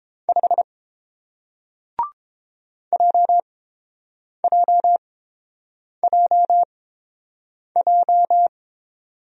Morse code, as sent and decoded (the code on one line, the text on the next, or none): ..... / .--- .--- .--- .---
5 JJJJ